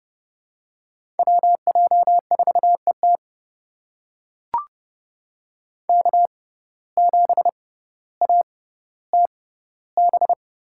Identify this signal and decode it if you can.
Morse code, heard as a keyed tone, about 30 wpm, spaced as for 15 wpm, 700 Hz. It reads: WJ4ET K7ATB
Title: USA Call Signs - 5 Characters - 6x Character Spacing - 30wpm